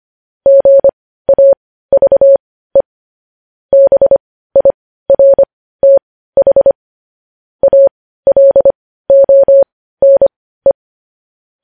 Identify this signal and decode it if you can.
Morse code, heard as a keyed tone, about 25 wpm, spaced as for 15 wpm, 550 Hz. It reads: GAVE BIRTH ALONE
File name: podcast / News headlines-1x-25x15